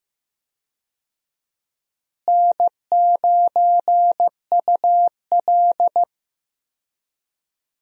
Morse code, heard as a keyed tone, about 15 wpm, 700 Hz. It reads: N9UL
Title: USA Call Signs - 4 Characters - Repeated 3x - 15wpm